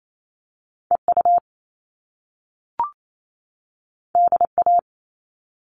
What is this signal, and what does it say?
Morse code, heard as a keyed tone, about 28 wpm, 700 Hz. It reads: EU DA
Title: All Cut Numbers - 2-Digit 28wpm